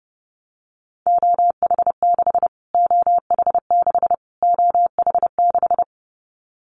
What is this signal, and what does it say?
Morse code, heard as a keyed tone, about 30 wpm, 700 Hz. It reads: OH6 OH6 OH6
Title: Callsign Prefixes for Top 25 DXCC - Repeated 3x - 30wpm